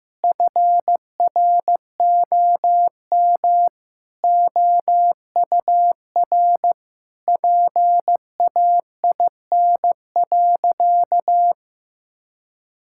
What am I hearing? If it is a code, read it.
Morse code, heard as a keyed tone, about 15 wpm, 700 Hz. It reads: FROM OUR PAIN.
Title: Sentences from Top 1600 Words 15wpm